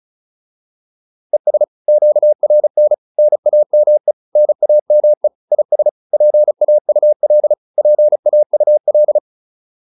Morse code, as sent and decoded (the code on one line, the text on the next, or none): . ... / --.- .-. -. / -. .- -- . / -. .- -- . / .. ... / .--. .- ..- .-.. / .--. .- ..- .-..
ES QRN NAME NAME IS PAUL PAUL